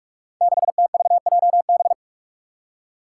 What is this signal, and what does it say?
Morse code, heard as a keyed tone, about 44 wpm, 700 Hz. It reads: 6TVJB
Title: Five Challenging Letter-Number Combinations 45wpm